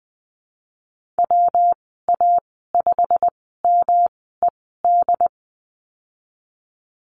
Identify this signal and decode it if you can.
Morse code, heard as a keyed tone, about 20 wpm, spaced as for 14 wpm, 700 Hz. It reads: WA5MED
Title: 122 Lesson 20g - Number 3 - Call Signs 20wpm